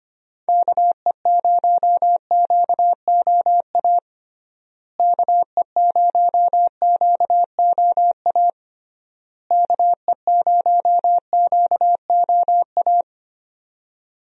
Morse code, heard as a keyed tone, about 25 wpm, 700 Hz. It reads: KE0QOA KE0QOA KE0QOA
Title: USA Call Signs - 6 Characters - Repeated 3x - 25wpm